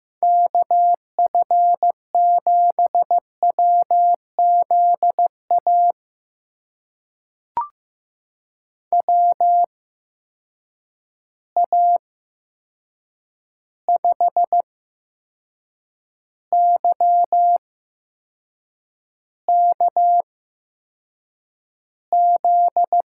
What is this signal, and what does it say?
Morse code, heard as a keyed tone, about 15 wpm, spaced as for 6 wpm, 700 Hz. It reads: KF7WZA WA5YKZ
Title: USA Call Signs - 6 Characters - 8x Character Spacing - 15wpm